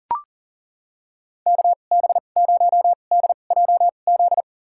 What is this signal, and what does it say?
Morse code, heard as a keyed tone, about 40 wpm, spaced as for 29 wpm, 700 Hz. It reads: KB0DJZ
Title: USA Call Signs - 2x Character Spacing - 40wpm